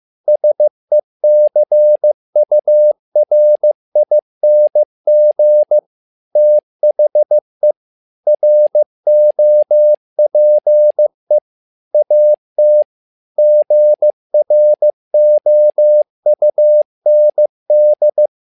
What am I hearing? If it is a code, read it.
Morse code, heard as a keyed tone, about 15 wpm, 600 Hz. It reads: SECURING THE ROPE AT GROUND
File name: learning center / tower trouble / tower trouble 15wpm